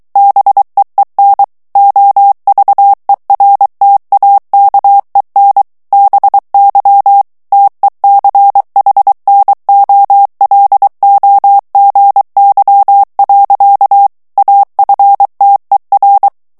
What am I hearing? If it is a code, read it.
Morse code, heard as a keyed tone, about 23 wpm, 800 Hz. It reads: BEEN OVERTAKEN BY TECHNOLOGY. AFTER
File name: ANGLAIS texte clair 3 vitesse 22 mots minute